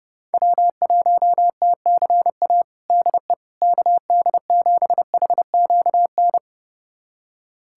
Morse code, heard as a keyed tone, about 32 wpm, 690 Hz. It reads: W1TCA DE KD7HQN
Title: USA Callsign DE Callsign 30wpm